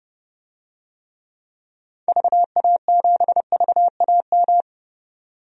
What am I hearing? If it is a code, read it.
Morse code, heard as a keyed tone, about 29 wpm, 700 Hz. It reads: VA7VAM
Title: DXCC-100 Call Signs - 6 Characters - Repeated 2x - 30wpm